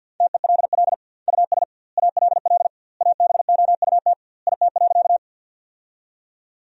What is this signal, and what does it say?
Morse code, heard as a keyed tone, about 47 wpm, 700 Hz. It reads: TELL US ALL ABOUT IT.
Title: 2x Word-Spacing - Sentences from Top 200 Words - 50wpm